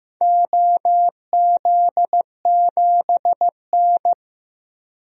Morse code, heard as a keyed tone, about 15 wpm, 700 Hz. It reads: OZ7N
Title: DXCC-100 Call Signs - 4 Characters - 15wpm